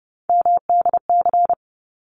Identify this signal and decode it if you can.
Morse code, heard as a keyed tone, about 30 wpm, 700 Hz. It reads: MDC
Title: ARRL Sections 30wpm